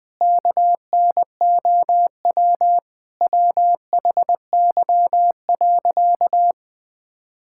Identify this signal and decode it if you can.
Morse code, heard as a keyed tone, about 19 wpm, 700 Hz. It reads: KNOW WHY.
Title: Sentences with 5-Word Phrases - 20wpm